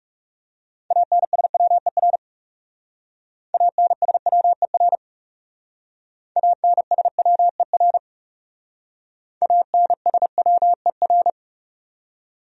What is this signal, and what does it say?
Morse code, heard as a keyed tone, about 37 wpm, 700 Hz. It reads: ANSWER ANSWER ANSWER ANSWER
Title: Speed-Racing Top 500 Words 30wpm